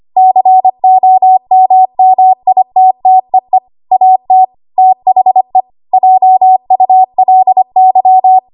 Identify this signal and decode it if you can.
Morse code, heard as a keyed tone, about 25 wpm, 750 Hz. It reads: COMMITTEE AT THE JULY
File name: Archive / 25 WPM / 25WPM